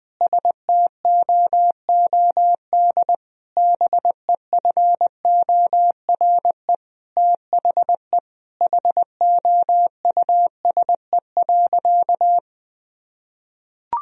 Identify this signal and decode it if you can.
Morse code, heard as a keyed tone, about 19 wpm, 700 Hz. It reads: STOOD BEFORE THE HOUSE.
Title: Sentences from Top 1000 Words - Review Part 2 - 20wpm